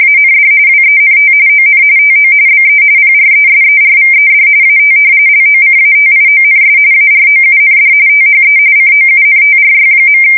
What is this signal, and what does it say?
RTTY, about 45.45 baud. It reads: AND WENT ON TO BE PRESERVED AT LOCATIONS AROUND THE COUNTRY.
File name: fsk170 / battleshipnj